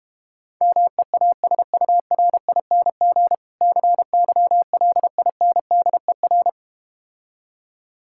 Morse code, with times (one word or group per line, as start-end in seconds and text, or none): 0.6-3.4 MEASURING
3.6-6.5 CYLINDER